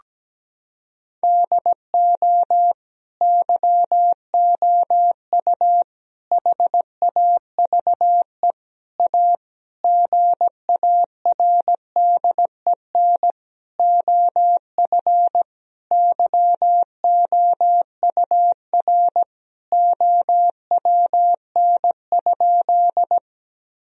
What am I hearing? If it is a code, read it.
Morse code, heard as a keyed tone, about 17 wpm, 700 Hz. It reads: DO YOU HAVE A GARDEN OF YOUR OWN?
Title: Sentences from Top 1000 Words - Review Part 4 - 17wpm